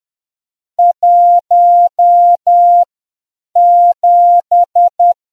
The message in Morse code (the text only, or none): 17